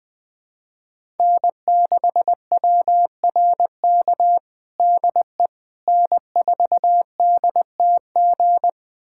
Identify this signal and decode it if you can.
Morse code, heard as a keyed tone, about 20 wpm, 700 Hz. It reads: N6WRK DE N4DTG
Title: USA Callsign DE Callsign 20wpm